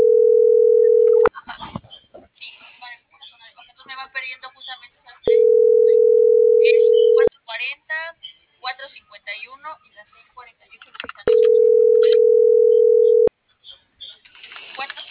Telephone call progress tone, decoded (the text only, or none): ringback tone